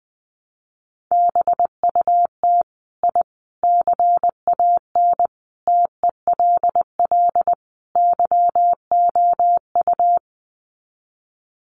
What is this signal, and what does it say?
Morse code, heard as a keyed tone, about 20 wpm, 700 Hz. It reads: BUT I CAN TELL YOU
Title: Sets of 5 Words Encore 20wpm